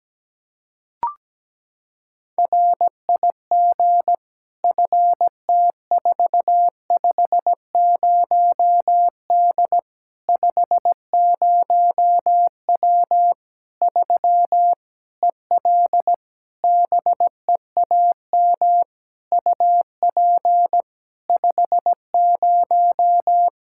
Morse code, heard as a keyed tone, about 17 wpm, 700 Hz. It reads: RIG FT450D 50W 3 EL BEAM UP 50
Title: K1USN SST QSO Practice 17wpm